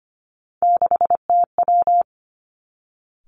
Morse code, heard as a keyed tone, about 25 wpm, 700 Hz. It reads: 6TW
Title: Three Challenging Letter-Number Combinations 25wpm